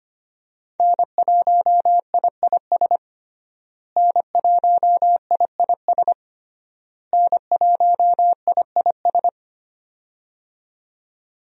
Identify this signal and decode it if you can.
Morse code, heard as a keyed tone, about 25 wpm, 700 Hz. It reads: N1IIS N1IIS N1IIS
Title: USA Call Signs - 5 Characters - Repeated 3x - 25wpm